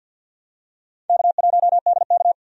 Morse code, heard as a keyed tone, about 49 wpm, 700 Hz. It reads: K1DK
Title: USA Call Signs - 4 Characters - 50wpm